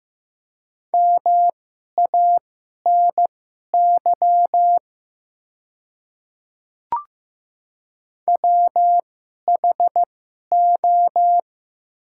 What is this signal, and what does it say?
Morse code, heard as a keyed tone, about 15 wpm, spaced as for 11 wpm, 700 Hz. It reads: MANY WHO
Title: 2x Character Spacing - Top 100 Words - 15wpm